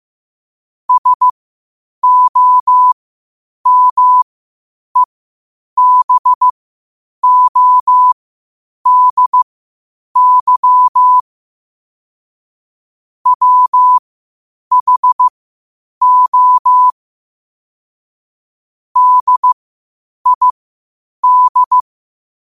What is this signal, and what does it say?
Morse code, heard as a keyed tone, about 15 wpm, spaced as for 8 wpm, 1000 Hz. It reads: SOMEBODY WHO DID
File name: qotdfiles / 8wpm